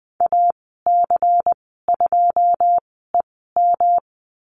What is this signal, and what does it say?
Morse code, heard as a keyed tone, about 20 wpm, spaced as for 15 wpm, 700 Hz. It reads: AC2EM
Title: 167 Lesson 27g - Number 7 - Call Signs 20wpm